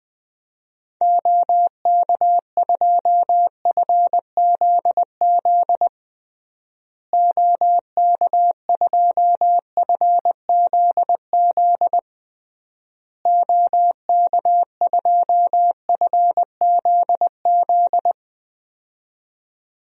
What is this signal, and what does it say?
Morse code, heard as a keyed tone, about 20 wpm, 700 Hz. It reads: OK2FZZ OK2FZZ OK2FZZ
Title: DXCC-100 Call Signs - 6 Characters - Repeated 3x - 20wpm